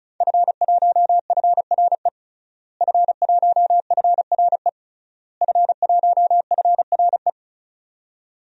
Morse code, heard as a keyed tone, about 36 wpm, 690 Hz. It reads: F1FRE F1FRE F1FRE
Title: DXCC-100 Call Signs - 5 Characters - Repeated 3x - 35wpm